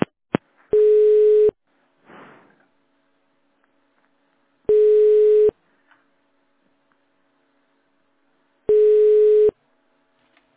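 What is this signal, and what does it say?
Ringback tone (425 Hz, 1 s on, 3 s off).
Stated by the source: Telephone tone recorded on the Petrosvyaz PBX.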